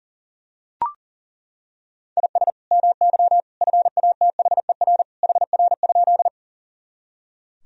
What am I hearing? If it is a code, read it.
Morse code, heard as a keyed tone, about 37 wpm, 700 Hz. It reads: IS MY FATHER HR?
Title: Sentences from Top 300 Words and Abbreviations 40wpm